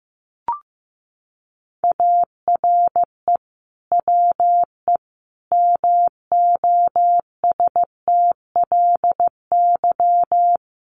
Morse code, heard as a keyed tone, about 15 wpm, 700 Hz. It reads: ARE WE MOSTLY
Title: Sentences from Top 1300 Words 15wpm